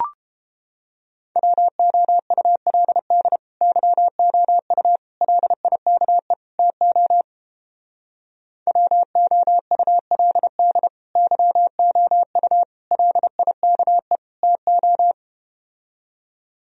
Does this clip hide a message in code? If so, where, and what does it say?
Yes: Morse code from 1.4 to 15.1 s, heard as a keyed tone, about 30 wpm, 700 Hz. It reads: WOULD YOU LIKE TO WOULD YOU LIKE TO